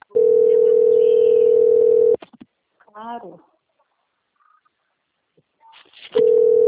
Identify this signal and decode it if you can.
Ringback tone (North America), 440 + 480 Hz, with speech.
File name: out- - - - - .60654